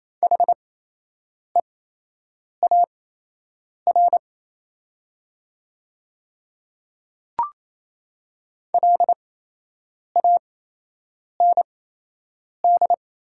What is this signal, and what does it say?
Morse code, heard as a keyed tone, about 27 wpm, spaced as for 8 wpm, 700 Hz. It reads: HEAR LAND
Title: 8x Character Spacing - Top 300 Words - 28wpm